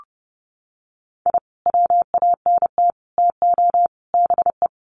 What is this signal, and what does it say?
Morse code, heard as a keyed tone, about 30 wpm, 700 Hz. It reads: I WANT TO BE
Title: Sets of 4 Words 30wpm